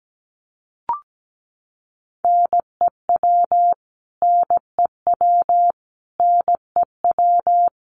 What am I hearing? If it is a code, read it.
Morse code, heard as a keyed tone, about 17 wpm, 700 Hz. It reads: NEW NEW NEW
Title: Top 200 Words Repeated 17wpm